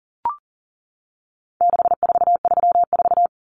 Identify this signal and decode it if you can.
Morse code, heard as a keyed tone, about 40 wpm, 700 Hz. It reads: 6434
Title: Numbers - 4-Digit 40wpm